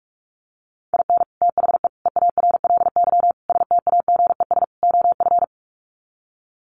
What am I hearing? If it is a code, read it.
Morse code, heard as a keyed tone, about 44 wpm, 700 Hz. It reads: IN THE EARLY STAGES OF
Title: Sets of 5 Words Encore 45wpm